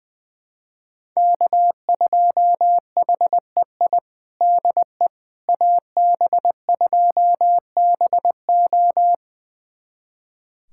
Morse code, heard as a keyed tone, about 21 wpm, 700 Hz. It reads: K2HEI DE AB2BO
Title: USA Callsign DE Callsign - 5 Characters - 20wpm